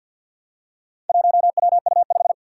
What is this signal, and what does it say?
Morse code, heard as a keyed tone, about 52 wpm, 700 Hz. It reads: 1WU5
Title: Four Challenging Letter-Number Combinations 50wpm